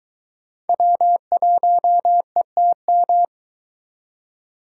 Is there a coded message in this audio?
Yes: Morse code from 0.7 to 3.2 s, heard as a keyed tone, about 23 wpm, 700 Hz. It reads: W1ETM